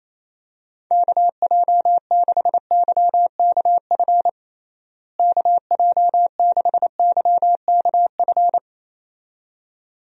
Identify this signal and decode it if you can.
Morse code, heard as a keyed tone, about 27 wpm, 700 Hz. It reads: KJ6YKF KJ6YKF
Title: USA Call Signs - 6 Characters - Repeated 2x - 28wpm